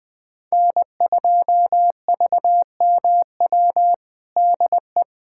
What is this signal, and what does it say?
Morse code, heard as a keyed tone, about 20 wpm, 700 Hz. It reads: N2VMW DE
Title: USA Callsign DE Callsign 20wpm